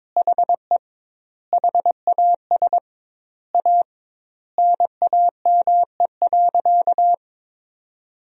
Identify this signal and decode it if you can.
Morse code, heard as a keyed tone, about 22 wpm, 700 Hz. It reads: HE HAS A NAME.
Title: 2x Word-Spacing - Sentences from Top 500 Words - 22wpm